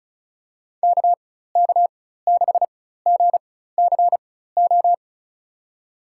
Morse code, heard as a keyed tone, about 36 wpm, spaced as for 16 wpm, 700 Hz. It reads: KK6GCO
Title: USA Call Signs - 6 Characters - 4x Character Spacing - 35wpm